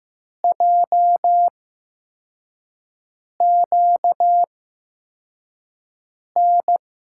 Morse code, heard as a keyed tone, about 15 wpm, spaced as for 4 wpm, 700 Hz. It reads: JQN
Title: USA Call Signs - 6 Characters - 8x Character Spacing - 15wpm